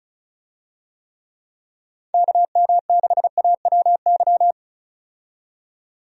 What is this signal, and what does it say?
Morse code, heard as a keyed tone, about 36 wpm, 700 Hz. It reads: KM6AWY
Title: USA Call Signs - 6 Characters - 35wpm